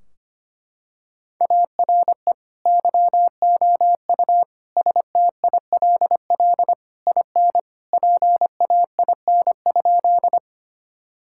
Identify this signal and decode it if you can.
Morse code, heard as a keyed tone, about 25 wpm, 700 Hz. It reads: ARE YOU STILL IN PAIN?
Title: Sentences from Top 1600 Words 25wpm